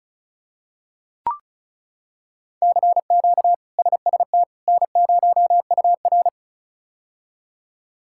Morse code, heard as a keyed tone, about 36 wpm, 700 Hz. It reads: CQ SST N0UR
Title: K1USN SST QSO Practice - 35wpm